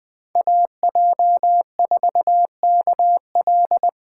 Morse code, heard as a keyed tone, about 20 wpm, 700 Hz. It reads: AJ4KL
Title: USA Call Signs - 3x Character Spacing - 20wpm